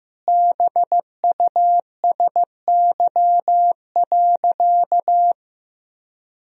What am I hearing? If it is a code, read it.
Morse code, heard as a keyed tone, about 15 wpm, 700 Hz. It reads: BUSY.
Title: Sentences from Top 1000 Words - Review Part 3 - 15wpm